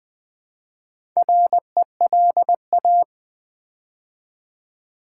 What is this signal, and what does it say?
Morse code, heard as a keyed tone, about 21 wpm, 700 Hz. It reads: RELA
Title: Build A Word - 4 Letters At A Time - 12-Letter Words - 20wpm